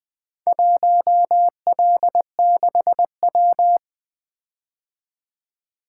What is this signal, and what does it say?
Morse code, heard as a keyed tone, about 20 wpm, 700 Hz. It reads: 1L6W
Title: Four Letter-Number Combinations - 20wpm